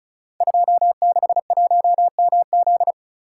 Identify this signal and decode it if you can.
Morse code, heard as a keyed tone, about 36 wpm, 700 Hz. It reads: 261MZ
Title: Code Groups - 5 Letter:Numbers - 35wpm